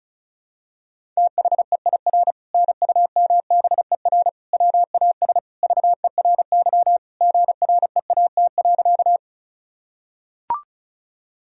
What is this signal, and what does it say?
Morse code, heard as a keyed tone, about 36 wpm, 700 Hz. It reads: THEIR NUMBER WAS VERY GREAT.
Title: Sentences from Top 1000 Words - Review Part 1 - 35wpm